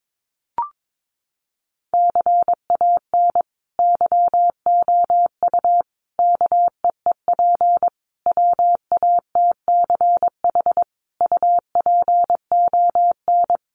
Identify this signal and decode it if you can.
Morse code, heard as a keyed tone, about 22 wpm, 700 Hz. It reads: CAN YOU KEEP WATCH UPON
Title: Sentences from Top 1100 Words 22wpm